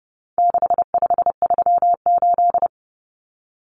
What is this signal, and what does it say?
Morse code, heard as a keyed tone, about 30 wpm, 700 Hz. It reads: 6538
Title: Common Cut Numbers - 4-Digit 30wpm